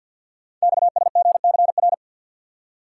Morse code, heard as a keyed tone, about 46 wpm, 700 Hz. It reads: XSGCR
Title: Code Groups - 5 Letters - 50wpm